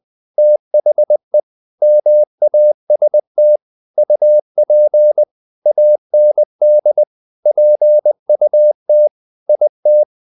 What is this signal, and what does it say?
Morse code, heard as a keyed tone, about 20 wpm, 600 Hz. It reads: THE MAST UP AND PUT IT
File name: the old man and the sea part 2 20wpm